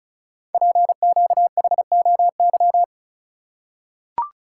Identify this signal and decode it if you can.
Morse code, heard as a keyed tone, about 36 wpm, 700 Hz. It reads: PQHOY
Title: Code Groups - 5 Letters - 35wpm